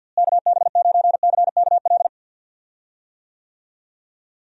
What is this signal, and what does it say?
Morse code, heard as a keyed tone, about 50 wpm, 700 Hz. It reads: KB9CKL
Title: USA Call Signs - 6 Characters - Repeated 2x - 50wpm